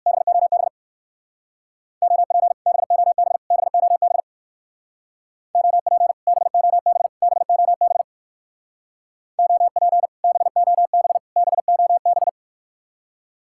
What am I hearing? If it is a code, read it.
Morse code, heard as a keyed tone, about 55 wpm, 700 Hz. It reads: BOB OP BOB BOB OP BOB BOB OP BOB BOB
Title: Speed-Racing Common QSO Phrases - 45wpm